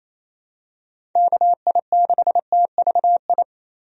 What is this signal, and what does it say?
Morse code, heard as a keyed tone, about 28 wpm, 700 Hz. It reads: KI6TVI
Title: USA Call Signs - 6 Characters - 28wpm